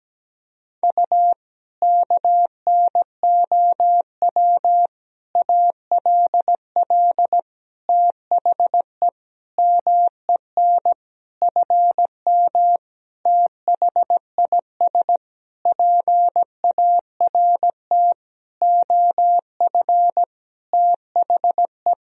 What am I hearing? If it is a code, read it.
Morse code, heard as a keyed tone, about 17 wpm, 700 Hz. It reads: U KNOW ALL THE MEN FM THIS PART OF THE
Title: Sentences from Top 300 Words and Abbreviations 17wpm